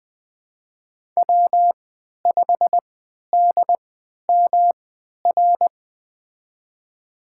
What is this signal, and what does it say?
Morse code, heard as a keyed tone, about 20 wpm, spaced as for 11 wpm, 700 Hz. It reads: W5DMR
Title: USA Call Signs - 5 Characters - 3x Character Spacing - 20wpm